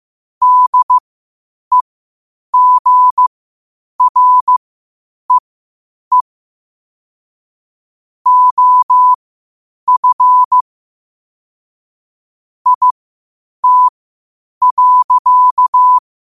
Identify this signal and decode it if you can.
Morse code, heard as a keyed tone, about 15 wpm, spaced as for 8 wpm, 1000 Hz. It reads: DEGREE OF IT.
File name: qotdfiles / 8wpm